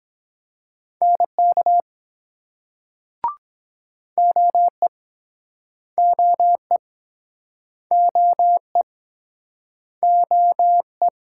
Morse code, heard as a keyed tone, about 24 wpm, 700 Hz. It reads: NK OE OE OE OE